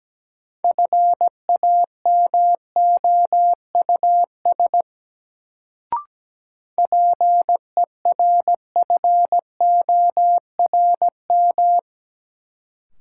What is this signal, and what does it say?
Morse code, heard as a keyed tone, about 17 wpm, 700 Hz. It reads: FAMOUS PERFORM
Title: Sentences from Top 1500 Words - New Word Drill Alt - 17wpm